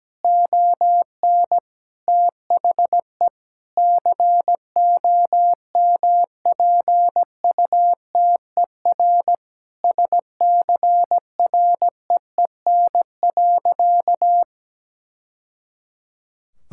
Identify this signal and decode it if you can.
Morse code, heard as a keyed tone, about 17 wpm, 700 Hz. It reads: ON THE COMPUTER SCREEN.
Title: Sentences from Top 1000 Words - Review Part 6 - 17wpm